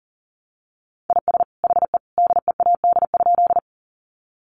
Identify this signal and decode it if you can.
Morse code, heard as a keyed tone, about 40 wpm, 700 Hz. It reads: IS HE DEAD?
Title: Sentences from Top 1500 Words 40wpm